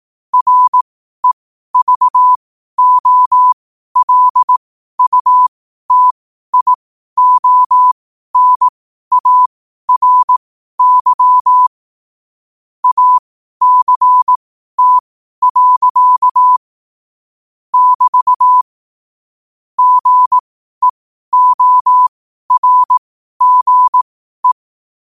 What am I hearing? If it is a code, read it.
Morse code, heard as a keyed tone, about 18 wpm, spaced as for 12 wpm, 1000 Hz. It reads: REVOLUTIONARY ACT. <BT> GEORGE
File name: qotdfiles / 12wpm